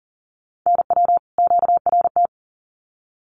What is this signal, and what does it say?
Morse code, heard as a keyed tone, about 40 wpm, 700 Hz. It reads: NW QRT
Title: Common QSO Phrases 40wpm